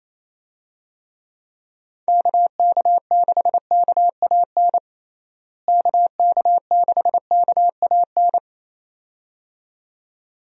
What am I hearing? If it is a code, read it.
Morse code, heard as a keyed tone, about 28 wpm, 700 Hz. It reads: KK6KAN KK6KAN
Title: USA Call Signs - 6 Characters - Repeated 2x - 28wpm